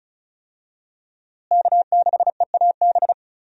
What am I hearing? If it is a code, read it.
Morse code, heard as a keyed tone, about 36 wpm, 700 Hz. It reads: K6EAB
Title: USA Call Signs - 5 Characters - Repeated 2x - 35wpm